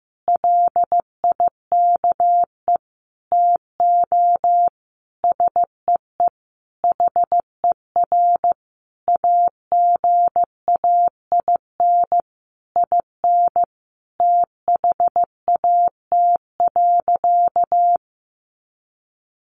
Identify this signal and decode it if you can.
Morse code, heard as a keyed tone, about 15 wpm, 700 Hz. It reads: LIKE TO SEE HER AGAIN IN THAT.
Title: Sentences from Top 200 Words 15wpm